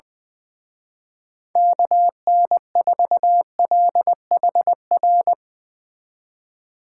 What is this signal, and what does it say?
Morse code, heard as a keyed tone, about 22 wpm, 700 Hz. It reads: KN4LHR
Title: Speed Racing - USA Call Signs 20wpm